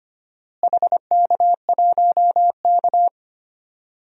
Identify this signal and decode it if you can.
Morse code, heard as a keyed tone, about 25 wpm, 700 Hz. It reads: HK1K
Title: DXCC-100 Call Signs - 4 Characters - Repeated 2x - 25wpm